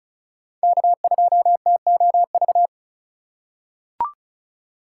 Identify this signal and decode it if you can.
Morse code, heard as a keyed tone, about 36 wpm, 690 Hz. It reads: K2TOV
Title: USA Call Signs - 5 Characters - 3x Character Spacing - 35wpm